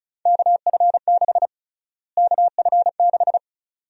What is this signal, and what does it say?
Morse code, heard as a keyed tone, about 36 wpm, 700 Hz. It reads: KF6 KF6
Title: USA Call Sign Prefixes - Repeated 2x - 35wpm